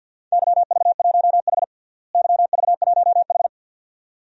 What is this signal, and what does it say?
Morse code, heard as a keyed tone, about 46 wpm, 700 Hz. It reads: YV1H YV1H
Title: DXCC-100 Call Signs - 4 Characters - Repeated 2x - 50wpm